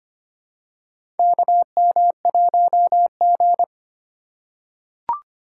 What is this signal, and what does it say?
Morse code, heard as a keyed tone, about 25 wpm, 700 Hz. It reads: KM1G
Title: USA Call Signs - 2x Character Spacing - 25wpm